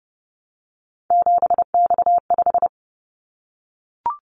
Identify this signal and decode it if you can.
Morse code, heard as a keyed tone, about 32 wpm, 700 Hz. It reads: 7X5
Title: Callsign Prefixes for Top 100 DXCC - 30wpm